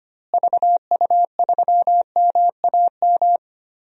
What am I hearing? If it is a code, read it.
Morse code, heard as a keyed tone, about 25 wpm, 700 Hz. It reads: VU3MAM
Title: DXCC-100 Call Signs - 6 Characters - Repeated 2x - 25wpm